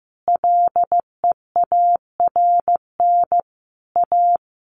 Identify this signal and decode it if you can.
Morse code, heard as a keyed tone, about 15 wpm, 700 Hz. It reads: LEARN A
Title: Sentences from Top 300 Words 15wpm